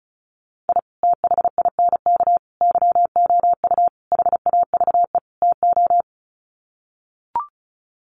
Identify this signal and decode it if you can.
Morse code, heard as a keyed tone, about 34 wpm, 700 Hz. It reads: I THINK YOU HAVE TO
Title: Sets of 5 Words 35wpm